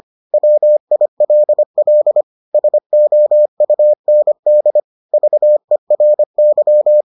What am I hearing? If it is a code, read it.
Morse code, heard as a keyed tone, about 25 wpm, 600 Hz. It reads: WILL SOUND VERY